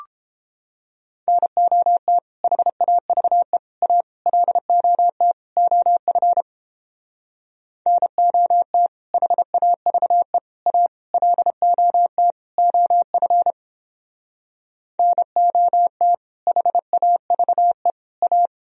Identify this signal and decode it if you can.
Morse code, heard as a keyed tone, about 29 wpm, 700 Hz. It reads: NOT HAVE A LOT OF NOT HAVE A LOT OF NOT HAVE A